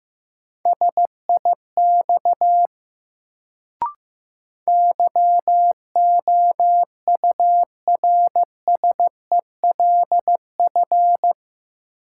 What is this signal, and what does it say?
Morse code, heard as a keyed tone, about 15 wpm, 700 Hz. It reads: SIX YOURSELF
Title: Sentences from Top 600 Words - New Word Drill - 15wpm